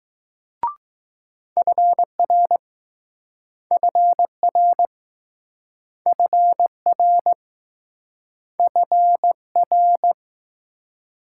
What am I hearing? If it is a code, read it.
Morse code, heard as a keyed tone, about 20 wpm, 700 Hz. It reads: FR FR FR FR